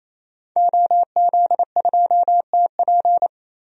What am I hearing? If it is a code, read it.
Morse code, heard as a keyed tone, about 28 wpm, 700 Hz. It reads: OZ2TP
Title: DXCC-100 Call Signs - 5 Characters - 28wpm